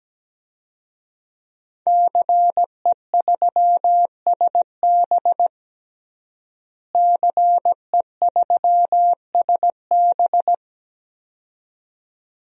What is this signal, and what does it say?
Morse code, heard as a keyed tone, about 17 wpm, 700 Hz. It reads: CE3SB CE3SB
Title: DXCC-100 Call Signs - 5 Characters - Repeated 2x - 17wpm